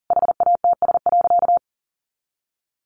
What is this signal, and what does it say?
Morse code, heard as a keyed tone, about 40 wpm, 700 Hz. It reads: HATS.
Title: Easy Phrases 40wpm